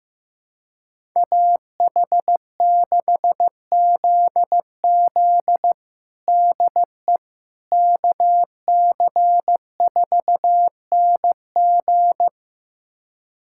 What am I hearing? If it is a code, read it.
Morse code, heard as a keyed tone, about 15 wpm, 700 Hz. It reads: AH6ZZ DE KC4NG
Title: USA Callsign DE Callsign - 5 Characters - 15wpm